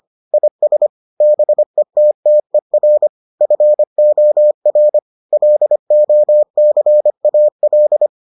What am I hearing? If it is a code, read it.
Morse code, heard as a keyed tone, about 25 wpm, 600 Hz. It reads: IS BETTER FOR LOCAL